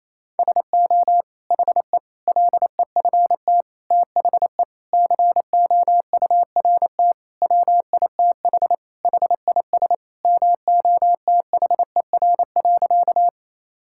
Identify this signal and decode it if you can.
Morse code, heard as a keyed tone, about 28 wpm, 700 Hz. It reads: SO HE LEFT THE COURT WITH HIS MOTHER.
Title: Sentences from Top 1600 Words 28wpm